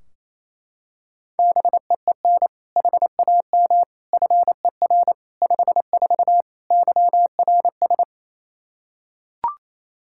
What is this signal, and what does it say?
Morse code, heard as a keyed tone, about 28 wpm, 700 Hz. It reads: BEEN HAM FER 54 YRS
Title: Practice QSO - Exchange 3 - More Info - 28wpm